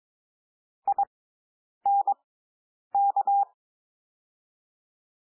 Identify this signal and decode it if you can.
Morse code, heard as a keyed tone, about 22 wpm, spaced as for 9 wpm, 800 Hz. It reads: INK
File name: SOUND WORDS 22 wpm